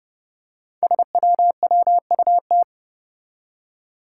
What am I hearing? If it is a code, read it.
Morse code, heard as a keyed tone, about 30 wpm, 700 Hz. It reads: SWWUT
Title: Five Challenging Letter-Number Combinations 30wpm